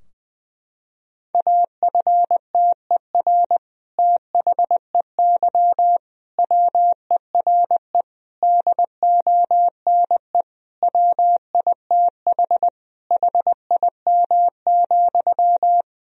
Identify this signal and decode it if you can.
Morse code, heard as a keyed tone, about 20 wpm, 700 Hz. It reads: AFTER THEY WERE DONE WITH HIM,
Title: Sentences from Top 1000 Words - Review Part 2 - 20wpm